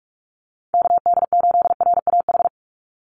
Morse code, heard as a keyed tone, about 44 wpm, 700 Hz. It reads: KD8RAH
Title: USA Call Signs 45wpm